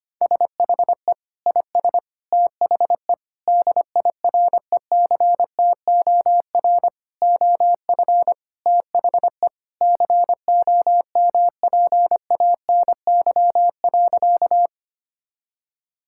Morse code, heard as a keyed tone, about 25 wpm, 700 Hz. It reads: SHE IS THE DIRECTOR OF THE COMPANY.
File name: practice / 25wpm / Sentences with 3-Word Phrases - 25wpm